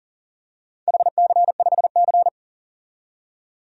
Morse code, heard as a keyed tone, about 40 wpm, 700 Hz. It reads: HC5C
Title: DXCC-100 Call Signs - 4 Characters - 40wpm